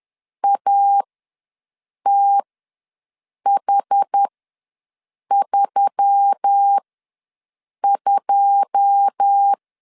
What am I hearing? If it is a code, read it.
Morse code, heard as a keyed tone, about 10 wpm, spaced as for 6 wpm, 800 Hz. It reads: ATH32
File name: LEZIONI / 9 2 ws 10wpm 4130